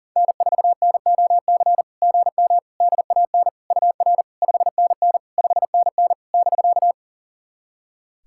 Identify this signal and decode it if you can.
Morse code, heard as a keyed tone, about 43 wpm, 700 Hz. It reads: N4NOC GM DAN UR 5NN 5NN <BK>
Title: SOTA QSO Practice 40wpm